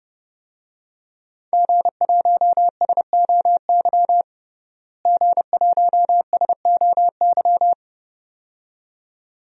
Morse code, heard as a keyed tone, about 30 wpm, 700 Hz. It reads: G1SOY G1SOY
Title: DXCC-100 Call Signs - 5 Characters - Repeated 2x - 30wpm